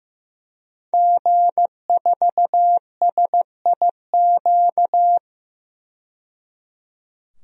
Morse code, heard as a keyed tone, about 16 wpm, 700 Hz. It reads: G4SIQ
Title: DXCC-100 Call Signs - 5 Characters - 15wpm